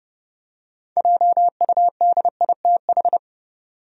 Morse code, heard as a keyed tone, about 30 wpm, 700 Hz. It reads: JUDITH
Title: USA Common Names - 30wpm